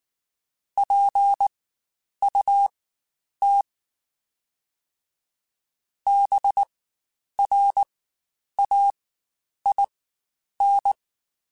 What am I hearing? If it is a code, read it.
Morse code, heard as a keyed tone, about 19 wpm, spaced as for 9 wpm, 800 Hz. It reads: PUT BRAIN
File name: Spaced / Chuckles Spaced-mp3s / BrainMouth sp @18